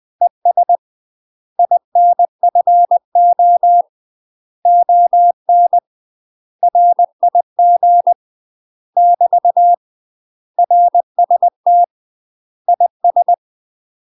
Morse code, heard as a keyed tone, about 20 wpm, 700 Hz. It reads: ES INFO ON RIG <BT> RST IS
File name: UK MASTER QSOs TRAINING/20WPM